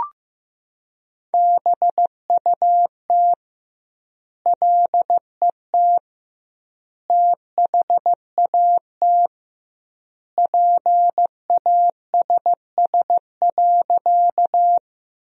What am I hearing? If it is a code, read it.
Morse code, heard as a keyed tone, about 15 wpm, 700 Hz. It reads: BUT LET THAT PASS.
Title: 2x Word-Spacing - Sentences from Top 400 Words - 15wpm